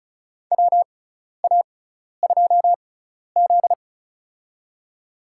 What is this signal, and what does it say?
Morse code, heard as a keyed tone, about 36 wpm, spaced as for 12 wpm, 700 Hz. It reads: WA2Z
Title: USA Call Signs - 6x Character Spacing - 35wpm